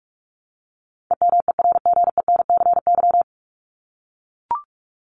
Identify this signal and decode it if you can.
Morse code, heard as a keyed tone, about 44 wpm, 700 Hz. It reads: EMERGENCY
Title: Top 1501-1750 Words 45wpm